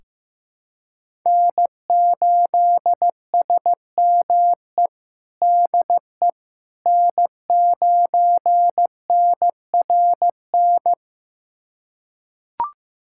Morse code, heard as a keyed tone, about 16 wpm, 700 Hz. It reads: N8SME DE N9NRN
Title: USA Callsign DE Callsign - 5 Characters - 15wpm